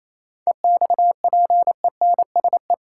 Morse code, heard as a keyed tone, about 27 wpm, 700 Hz. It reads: EXPENSE
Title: Sets of 4 Words - New Word Drill - 28wpm